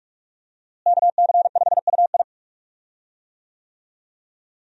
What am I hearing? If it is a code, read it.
Morse code, heard as a keyed tone, about 47 wpm, 700 Hz. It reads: KC5UI
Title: USA Call Signs - 5 Characters - Repeated 3x - 45wpm